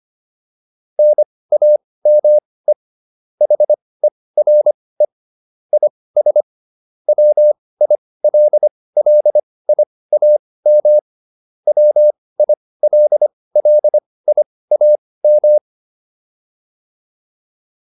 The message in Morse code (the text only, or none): NAME HERE IS WILLIAM WILLIAM